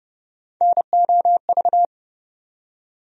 Morse code, heard as a keyed tone, about 30 wpm, 700 Hz. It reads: NOV
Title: USA Call Sign Postfixes - 3 Letters 30wpm